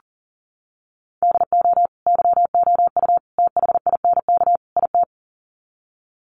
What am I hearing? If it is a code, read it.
Morse code, heard as a keyed tone, about 40 wpm, 700 Hz. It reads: DO YOU THINK IT
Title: Sets of 4 Words 40wpm